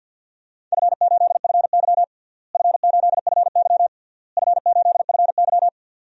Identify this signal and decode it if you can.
Morse code, heard as a keyed tone, about 52 wpm, 700 Hz. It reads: F8FY F8FY F8FY
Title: DXCC-100 Call Signs - 4 Characters - Repeated 3x - 50wpm